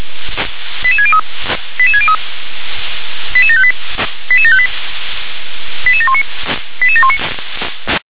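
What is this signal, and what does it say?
Five-tone selective call (ZVEI), heard as quick runs of five tones, tones of about 70 ms. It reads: 88603 88603 88758 88758 88719 88719
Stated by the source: ZVEI five-tone selective call 88714, received on an SDR.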